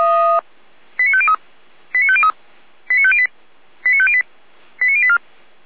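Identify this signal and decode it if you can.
Five-tone selective call (ZVEI), heard as quick runs of five tones, tones of about 70 ms. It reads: 89503 89503 89508 89508 89084
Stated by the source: ZVEI five-tone call received by SDR.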